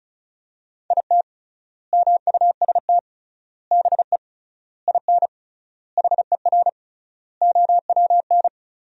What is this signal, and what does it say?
Morse code, heard as a keyed tone, about 36 wpm, 700 Hz. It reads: IT MUST BE IN HER OWN
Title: 3x Word-Spacing - Sentences from Top 300 Words - 35wpm